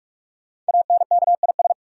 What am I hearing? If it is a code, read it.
Morse code, heard as a keyed tone, about 44 wpm, 700 Hz. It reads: ANKIS
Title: Code Groups - 5 Letters - 45wpm